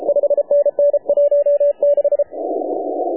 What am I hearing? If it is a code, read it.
Morse code, heard as a keyed tone, about 33 wpm, 560 Hz. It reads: 5NN 16
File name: AUDIO2019 / HA7UI CW